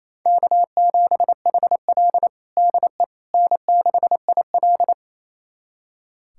Morse code, heard as a keyed tone, about 28 wpm, 690 Hz. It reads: K7HL DE N6IL
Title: USA Callsign DE Callsign - 4 Characters - 28wpm